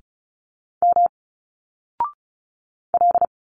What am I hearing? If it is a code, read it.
Morse code, heard as a keyed tone, about 34 wpm, 700 Hz. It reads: M L
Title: Single Letters 35wpm